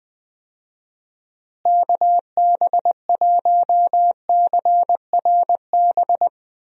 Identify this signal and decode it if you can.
Morse code, heard as a keyed tone, about 20 wpm, 700 Hz. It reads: KB1CRB
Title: USA Call Signs - 6 Characters - Repeated 2x - 20wpm